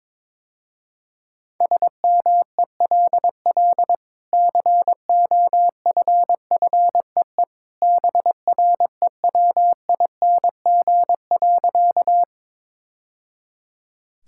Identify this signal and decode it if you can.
Morse code, heard as a keyed tone, about 22 wpm, 700 Hz. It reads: SMELL COFFEE BREWING.
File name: practice / 22wpm / Easy Sentences with Hints - 22wpm